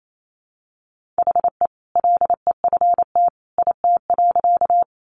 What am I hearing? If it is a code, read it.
Morse code, heard as a keyed tone, about 28 wpm, 700 Hz. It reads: HE LEFT IT.
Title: A Sound Of Thunder 28wpm